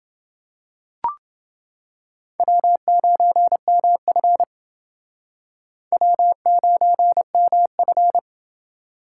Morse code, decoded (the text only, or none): W9MF W9MF